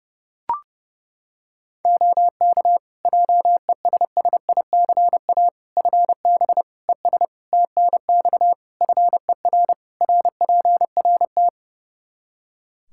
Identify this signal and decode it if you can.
Morse code, heard as a keyed tone, about 32 wpm, 700 Hz. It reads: OK JESSICA FB ES TNX FER RPRT
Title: Practice QSO - Exchange 2 - Info - 30wpm